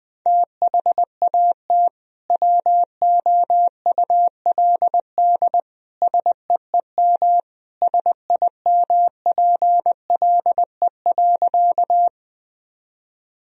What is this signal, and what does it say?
Morse code, heard as a keyed tone, about 21 wpm, 700 Hz. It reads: THAT WOULD SEEM SIMPLE.
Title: Sentences from Top 1000 Words - Review Part 1 - 20wpm